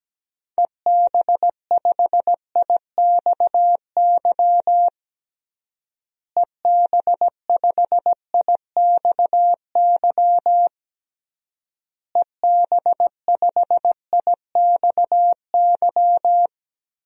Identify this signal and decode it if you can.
Morse code, heard as a keyed tone, about 17 wpm, 700 Hz. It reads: EB5IXY EB5IXY EB5IXY
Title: DXCC-100 Call Signs - 6 Characters - Repeated 3x - 17wpm